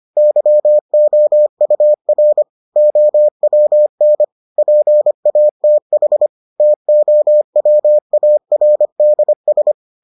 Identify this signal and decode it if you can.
Morse code, heard as a keyed tone, about 25 wpm, 600 Hz. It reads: YOUR OWN PATH TOWARDS